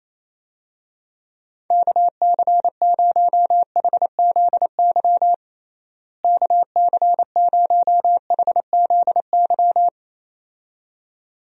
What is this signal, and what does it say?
Morse code, heard as a keyed tone, about 28 wpm, 700 Hz. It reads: KC0HZY KC0HZY
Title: USA Call Signs - 6 Characters - Repeated 2x - 28wpm